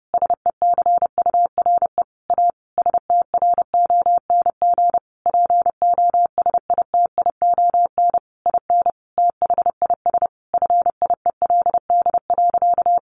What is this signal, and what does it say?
Morse code, heard as a keyed tone, about 30 wpm, 700 Hz. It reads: SECURE A STRONG POSITION IN THIS FIELD.
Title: Sentences from Top 1200 Words 30wpm